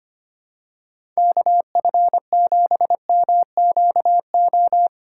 Morse code, heard as a keyed tone, about 25 wpm, 700 Hz. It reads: KF7MQO
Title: USA Call Signs - 6 Characters - 4x Character Spacing - 25wpm